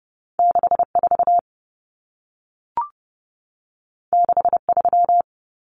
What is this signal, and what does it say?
Morse code, heard as a keyed tone, about 30 wpm, 700 Hz. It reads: 64 63
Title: Numbers - 2-Digit 30wpm